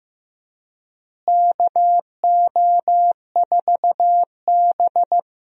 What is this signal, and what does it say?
Morse code, heard as a keyed tone, about 15 wpm, 700 Hz. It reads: KO4B
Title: USA Call Signs - 4 Characters - Repeated 3x - 15wpm